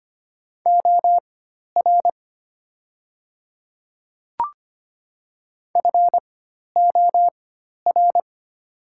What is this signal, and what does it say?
Morse code, heard as a keyed tone, about 25 wpm, spaced as for 12 wpm, 700 Hz. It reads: OR FOR
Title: 4x Character Spacing - Top 100 Words - 25wpm